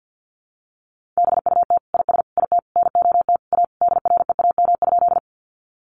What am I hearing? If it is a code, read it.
Morse code, heard as a keyed tone, about 50 wpm, 700 Hz. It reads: BUT IS IT NOT A DREAM?
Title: Sentences from Top 1500 Words 50wpm